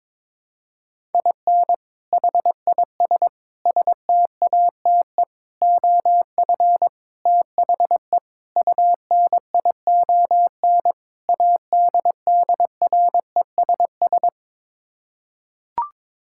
Morse code, heard as a keyed tone, about 21 wpm, 700 Hz. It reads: IN HIS STATE OF THE UNION ADDRESS
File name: practice / 22wpm / Sets of 7 Words - 22wpm